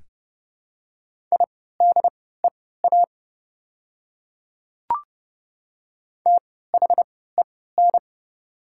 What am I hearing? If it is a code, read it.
Morse code, heard as a keyed tone, about 30 wpm, spaced as for 17 wpm, 690 Hz. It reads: IDEA THEN
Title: 3x Character Spacing - Top 300 Words - 30wpm